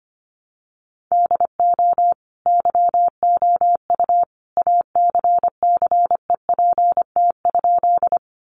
Morse code, heard as a keyed tone, about 25 wpm, 700 Hz. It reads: DO YOU ACCEPT?
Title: Sentences from Top 1300 Words 25wpm